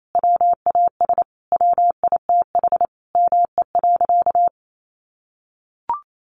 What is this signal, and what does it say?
Morse code, heard as a keyed tone, about 28 wpm, 700 Hz. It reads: WAS WITH ME.
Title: Sentences from Top 700 Words 28wpm